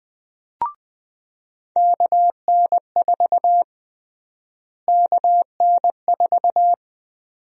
Morse code, heard as a keyed tone, about 20 wpm, 700 Hz. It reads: KN4 KN4
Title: USA Call Sign Prefixes - Repeated 2x - 20wpm